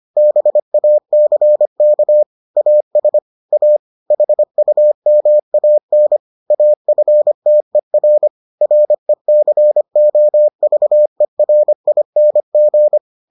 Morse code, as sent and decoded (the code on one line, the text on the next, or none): -... .- -.-. -.- / .- ... / .- / .... ..- -- .- -. / .- ..-. - . .-. / .-. -.-. --- ...- . .-. .. -. --.
BACK AS A HUMAN AFTER RECOVERING